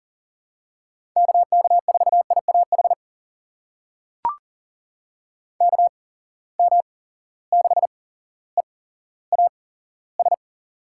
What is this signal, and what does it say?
Morse code, heard as a keyed tone, about 40 wpm, spaced as for 17 wpm, 700 Hz. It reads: KK4IAH KM6EAS
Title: USA Call Signs - 6 Characters - 8x Character Spacing - 40wpm